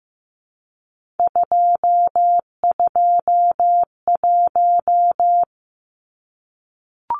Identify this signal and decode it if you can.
Morse code, heard as a keyed tone, about 15 wpm, 700 Hz. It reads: 221
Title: Numbers - 3-Digit 15wpm